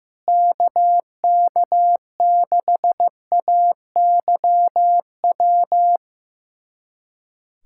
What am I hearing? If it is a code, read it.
Morse code, heard as a keyed tone, about 16 wpm, 700 Hz. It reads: KK6AYW
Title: USA Callsign DE Callsign - 6 Characters - 15wpm